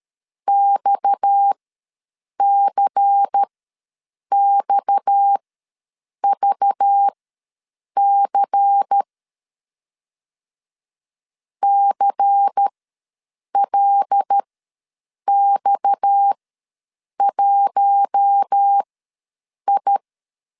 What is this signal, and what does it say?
Morse code, heard as a keyed tone, about 13 wpm, spaced as for 7 wpm, 800 Hz. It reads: XCXVC CLX1I
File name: LEZIONI / 6 PKX0 / 6 PKX ws 12wpm 3106